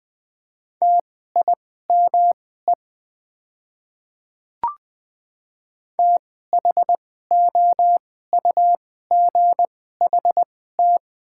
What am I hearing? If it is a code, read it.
Morse code, heard as a keyed tone, about 20 wpm, spaced as for 14 wpm, 700 Hz. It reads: TIME THOUGHT
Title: 2x Character Spacing - Top 500 Words - 20wpm